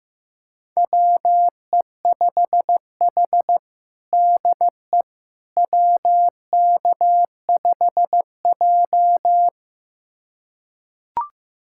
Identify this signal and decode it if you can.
Morse code, heard as a keyed tone, about 15 wpm, 700 Hz. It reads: WE5H DE WK5J
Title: USA Callsign DE Callsign - 4 Characters - 15wpm